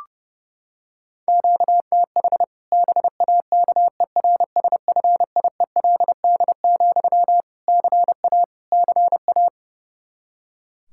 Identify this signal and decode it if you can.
Morse code, heard as a keyed tone, about 31 wpm, 700 Hz. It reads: QTH BAKERSFIELD, CA CA
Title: Practice QSO - Exchange 1 - Beginning - 30wpm